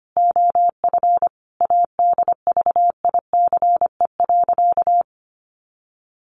Morse code, heard as a keyed tone, about 25 wpm, 700 Hz. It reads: OF ADVICE.
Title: Sentences from Top 1100 Words 25wpm